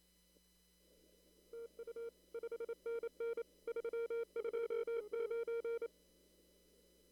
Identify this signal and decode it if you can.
Morse code, heard as a keyed tone, about 27 wpm, 460 Hz. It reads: TU 5NN 329
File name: CW / kt5j wpx cw-op-k5pi